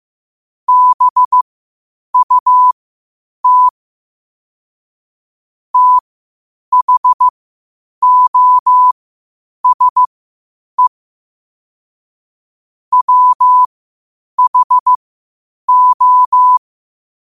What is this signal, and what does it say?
Morse code, heard as a keyed tone, about 15 wpm, spaced as for 8 wpm, 1000 Hz. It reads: BUT THOSE WHO
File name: 8wpm